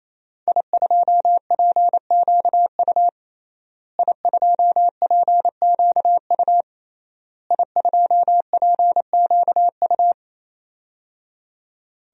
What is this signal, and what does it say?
Morse code, heard as a keyed tone, about 28 wpm, 700 Hz. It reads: I2PQU I2PQU I2PQU
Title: DXCC-100 Call Signs - 5 Characters - Repeated 3x - 28wpm